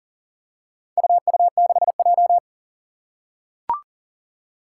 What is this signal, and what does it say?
Morse code, heard as a keyed tone, about 40 wpm, 700 Hz. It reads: UU6J
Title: Four Challenging Letter-Number Combinations 40wpm